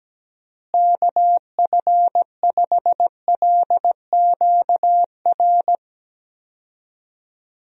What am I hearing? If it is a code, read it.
Morse code, heard as a keyed tone, about 17 wpm, 700 Hz. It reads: KF5LQR
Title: USA Call Signs - 6 Characters - Repeated 3x - 17wpm